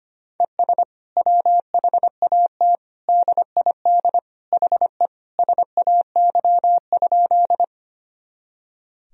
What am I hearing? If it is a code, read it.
Morse code, heard as a keyed tone, about 25 wpm, 700 Hz. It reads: ES WHAT DID HE SAY?
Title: Sentences from Top 200 Words and Abbreviations 25wpm